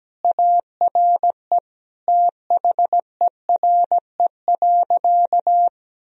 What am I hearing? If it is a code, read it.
Morse code, heard as a keyed tone, about 17 wpm, 700 Hz. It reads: ARE THERE.
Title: Sentences from Top 1000 Words - Review Part 1 - 17wpm